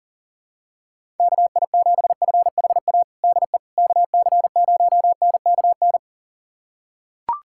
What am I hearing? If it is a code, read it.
Morse code, heard as a keyed tone, about 40 wpm, 700 Hz. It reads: KI7FHA DE KC0NKN
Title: USA Callsign DE Callsign 40wpm